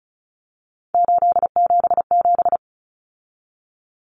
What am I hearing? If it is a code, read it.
Morse code, heard as a keyed tone, about 34 wpm, 700 Hz. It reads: 877
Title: Common Cut Numbers - 3-Digit 35wpm